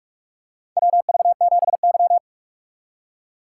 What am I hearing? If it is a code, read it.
Morse code, heard as a keyed tone, about 47 wpm, 700 Hz. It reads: WV7Y
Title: USA Call Signs - 4 Characters - 4x Character Spacing - 45wpm